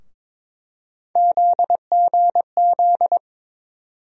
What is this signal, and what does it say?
Morse code, heard as a keyed tone, about 21 wpm, 700 Hz. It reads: ZGZ
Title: USA Call Sign Postfixes - 3 Letters 22wpm